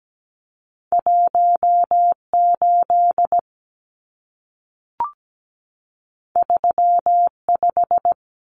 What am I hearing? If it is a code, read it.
Morse code, heard as a keyed tone, about 17 wpm, 700 Hz. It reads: 18 35
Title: Numbers - 2-Digit 17wpm